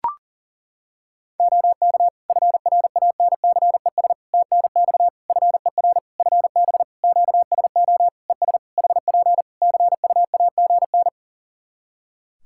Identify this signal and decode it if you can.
Morse code, heard as a keyed tone, about 40 wpm, 700 Hz. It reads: OK FRANCES TNX FER FB QSO ES HP CUAGN
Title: Practice QSO - Exchange 4 - Ending - 40wpm